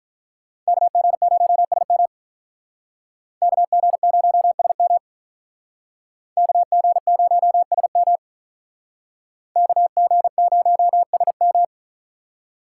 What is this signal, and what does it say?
Morse code, heard as a keyed tone, about 44 wpm, 700 Hz. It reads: KG0SM KG0SM KG0SM KG0SM